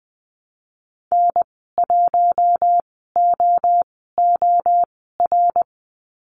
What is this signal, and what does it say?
Morse code, heard as a keyed tone, about 20 wpm, spaced as for 14 wpm, 700 Hz. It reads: N1OOR
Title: 076 Lesson 13g - Letter L - Call Signs 20wpm